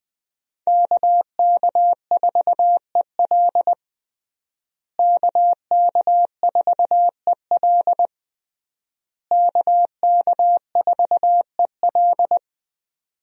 Morse code, heard as a keyed tone, about 21 wpm, 700 Hz. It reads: KK4EL KK4EL KK4EL
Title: USA Call Signs - 5 Characters - Repeated 3x - 20wpm